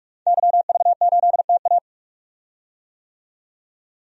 Morse code, heard as a keyed tone, about 46 wpm, 700 Hz. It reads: YV8TA
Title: DXCC-100 Call Signs - 5 Characters - 45wpm